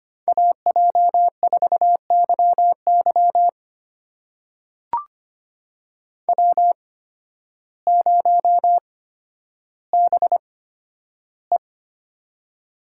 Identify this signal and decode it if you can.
Morse code, heard as a keyed tone, about 25 wpm, spaced as for 12 wpm, 700 Hz. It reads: AJ4YY W0BE
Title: USA Call Signs - 8x Character Spacing - 25wpm